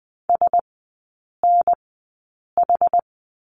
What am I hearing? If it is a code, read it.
Morse code, heard as a keyed tone, about 20 wpm, spaced as for 8 wpm, 700 Hz. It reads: SNH
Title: 060 Lesson 11e - Letter h - Three Characters 20wpm